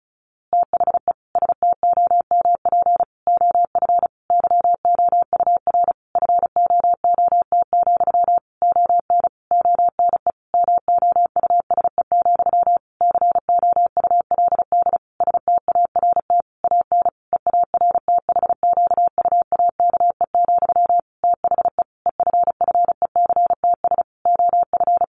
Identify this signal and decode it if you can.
Morse code, heard as a keyed tone, about 34 wpm, 700 Hz. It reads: THE STOMP OF YOUR FOOT, ON ONE MOUSE, COULD START AN EARTHQUAKE, THE EFFECTS OF
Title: A Sound Of Thunder 35wpm